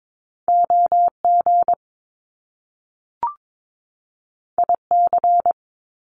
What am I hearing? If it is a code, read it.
Morse code, heard as a keyed tone, about 22 wpm, 700 Hz. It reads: OG IC
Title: Top Two Letter Combinations in Words 22wpm